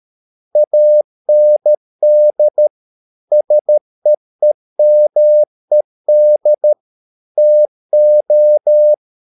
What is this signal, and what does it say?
Morse code, heard as a keyed tone, about 13 wpm, 600 Hz. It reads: AND SEEMED TO